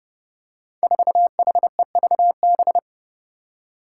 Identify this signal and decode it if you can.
Morse code, heard as a keyed tone, about 32 wpm, 700 Hz. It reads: 4HEVB
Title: Five Challenging Letter-Number Combinations 30wpm